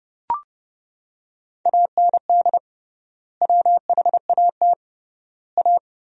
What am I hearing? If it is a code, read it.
Morse code, heard as a keyed tone, about 30 wpm, 700 Hz. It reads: AND WHAT A
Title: 3x Word-Spacing - Sentences from Top 500 Words - 30wpm